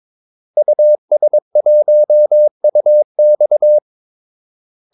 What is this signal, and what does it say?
Morse code, heard as a keyed tone, about 22 wpm, 600 Hz. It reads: US1UX
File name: Callsigns-22wpm